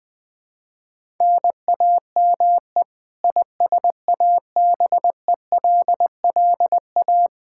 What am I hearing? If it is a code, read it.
Morse code, heard as a keyed tone, about 20 wpm, 700 Hz. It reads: NAME ISABELLA
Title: Practice QSO - Exchange 1 - Beginning - 20wpm